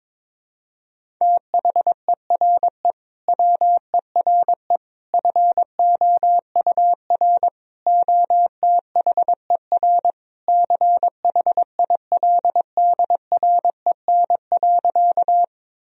Morse code, heard as a keyed tone, about 21 wpm, 700 Hz. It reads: THERE WERE FOUR OTHER CHILDREN.
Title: Sentences from Top 1000 Words - Review Part 1 - 22wpm